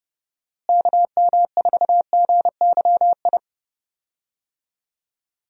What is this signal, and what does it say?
Morse code, heard as a keyed tone, about 31 wpm, 700 Hz. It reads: KM4GYI
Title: USA Call Signs - 6 Characters - Repeated 2x - 30wpm